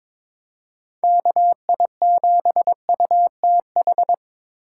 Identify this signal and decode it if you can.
Morse code, heard as a keyed tone, about 22 wpm, 700 Hz. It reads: KI7UTH
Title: USA Call Signs - 6 Characters - 3x Character Spacing - 22wpm